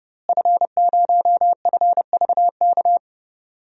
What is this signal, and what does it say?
Morse code, heard as a keyed tone, about 30 wpm, 700 Hz. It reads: F0FVK
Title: DXCC-100 Call Signs - 5 Characters - Repeated 3x - 30wpm